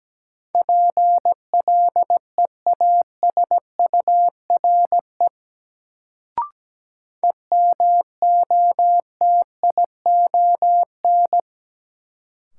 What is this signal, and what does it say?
Morse code, heard as a keyed tone, about 17 wpm, 700 Hz. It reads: PLEASURE EMOTION
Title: Sentences from Top 1600 Words - New Word Drill Alt - 17wpm